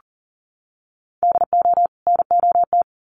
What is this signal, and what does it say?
Morse code, heard as a keyed tone, about 40 wpm, 700 Hz. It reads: DO NOT
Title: Sets of 2 Words 40wpm